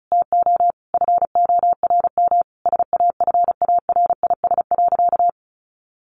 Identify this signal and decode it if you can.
Morse code, heard as a keyed tone, about 34 wpm, 700 Hz. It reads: TO FORM SAFARIS.
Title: A Sound Of Thunder 35wpm